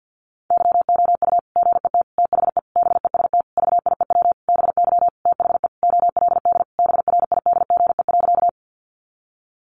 Morse code, heard as a keyed tone, about 50 wpm, 700 Hz. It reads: YOU GET THE BEST VIEW BY THE OLD BRIDGE.
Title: Sentences from Top 1200 Words 50wpm